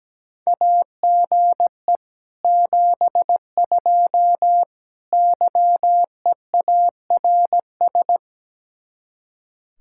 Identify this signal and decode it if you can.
Morse code, heard as a keyed tone, about 17 wpm, 700 Hz. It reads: AGE 72 YEARS
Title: Speed-Racing Common QSO Phrases - 17wpm